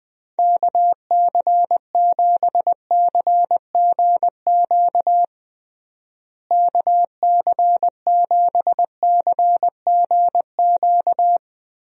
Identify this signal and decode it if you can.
Morse code, heard as a keyed tone, about 21 wpm, 700 Hz. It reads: KC7CGQ KC7CGQ
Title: USA Call Signs - 6 Characters - Repeated 2x - 20wpm